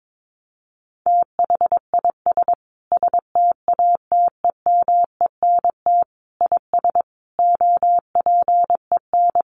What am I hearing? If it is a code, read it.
Morse code, heard as a keyed tone, about 22 wpm, 700 Hz. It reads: THIS STATEMENT IS OPEN
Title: Sentences from Top 1300 Words 22wpm